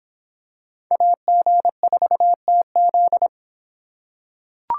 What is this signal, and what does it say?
Morse code, heard as a keyed tone, about 25 wpm, 700 Hz. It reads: AG4TZ